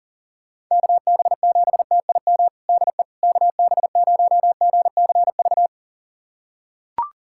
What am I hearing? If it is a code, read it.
Morse code, heard as a keyed tone, about 40 wpm, 700 Hz. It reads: KB7TIM DE KB0GCV
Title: USA Callsign DE Callsign - 6 Characters - 40wpm